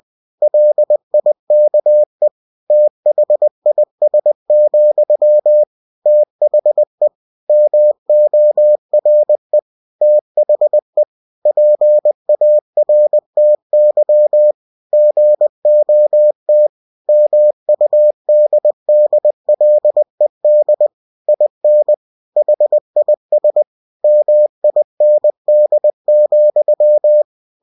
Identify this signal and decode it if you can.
Morse code, heard as a keyed tone, about 20 wpm, 600 Hz. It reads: LIKE THIS, THE MORE THE PARTY GOT MUDDLED IN HIS MIND,
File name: cw-books / winnie-the-pooh / winnie-20wpm-page-0103